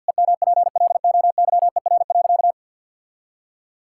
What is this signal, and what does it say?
Morse code, heard as a keyed tone, about 50 wpm, 700 Hz. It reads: EMPLOYER.
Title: Sentences from Top 1600 Words 50wpm